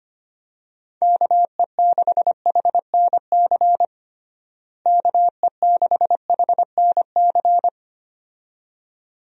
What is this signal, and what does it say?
Morse code, heard as a keyed tone, about 25 wpm, 700 Hz. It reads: KE6HNC KE6HNC
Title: USA Call Signs - 6 Characters - Repeated 2x - 25wpm